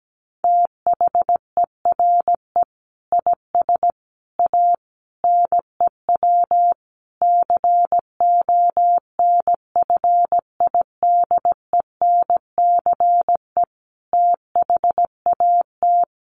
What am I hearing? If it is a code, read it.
Morse code, heard as a keyed tone, about 17 wpm, 700 Hz. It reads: THERE IS A NEW CONFIDENCE THAT
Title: Sentences from Top 1500 Words 17wpm